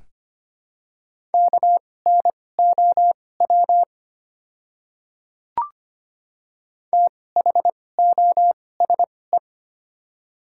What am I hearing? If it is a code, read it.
Morse code, heard as a keyed tone, about 25 wpm, spaced as for 18 wpm, 700 Hz. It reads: KNOW THOSE
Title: 2x Character Spacing - Top 500 Words - 25wpm